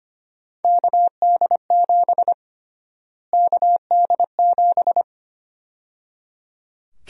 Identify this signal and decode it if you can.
Morse code, heard as a keyed tone, about 25 wpm, 700 Hz. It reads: KD7 KD7
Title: USA Call Sign Prefixes - Repeated 2x - 25wpm